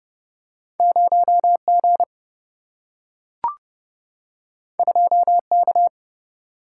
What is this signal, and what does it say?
Morse code, heard as a keyed tone, about 30 wpm, 700 Hz. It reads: 0 G 2 K
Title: Single-Letter Single-Number 30wpm